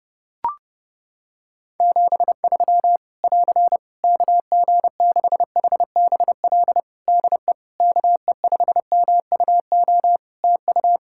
Morse code, heard as a keyed tone, about 30 wpm, 700 Hz. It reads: 73 <AR> KG6HBL DE KE5MUO TU
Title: Practice QSO - Exchange 4 - Ending - 30wpm